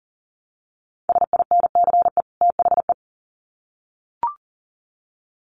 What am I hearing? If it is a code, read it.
Morse code, heard as a keyed tone, about 40 wpm, 700 Hz. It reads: SINCE THE
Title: Sets of 2 Words 40wpm